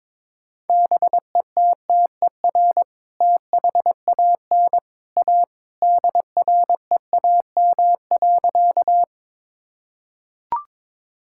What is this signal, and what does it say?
Morse code, heard as a keyed tone, about 22 wpm, 700 Hz. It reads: BETTER THAN A DREAM.
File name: practice / 22wpm / Sentences with 2-Word Phrases - 22wpm